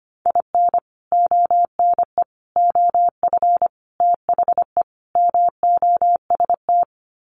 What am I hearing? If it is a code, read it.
Morse code, heard as a keyed tone, about 25 wpm, 700 Hz. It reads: IN ONE OF THE MOST
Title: Sets of 5 Words Encore 25wpm